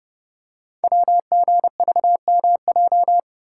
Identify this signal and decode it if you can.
Morse code, heard as a keyed tone, about 30 wpm, 700 Hz. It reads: WGVMJ
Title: Code Groups - 5 Letters - 30wpm